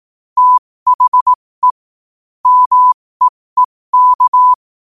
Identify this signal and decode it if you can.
Morse code, heard as a keyed tone, about 18 wpm, spaced as for 15 wpm, 1000 Hz. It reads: THE MEEK
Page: New quotes every day in morse code at 15 Words per minute.